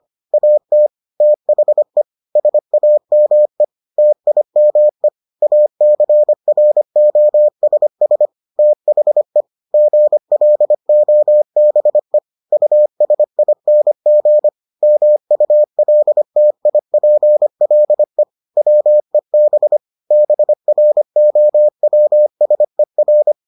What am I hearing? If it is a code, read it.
Morse code, heard as a keyed tone, about 25 wpm, 600 Hz. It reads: AT THE SAME TIME ACROSS THE GLOBE USING MULTIPLE WEB BROWSER